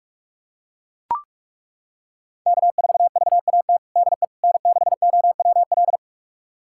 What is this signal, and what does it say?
Morse code, heard as a keyed tone, about 41 wpm, 700 Hz. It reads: K4VAT DE N6OWL
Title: USA Callsign DE Callsign - 5 Characters - 45wpm